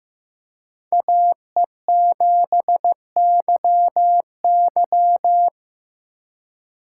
Morse code, heard as a keyed tone, about 15 wpm, 700 Hz. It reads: AE7YY
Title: USA Call Signs - 4x Character Spacing - 15wpm